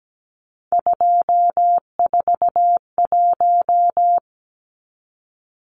Morse code, heard as a keyed tone, about 17 wpm, 700 Hz. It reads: 241
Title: Numbers - 3-Digit 17wpm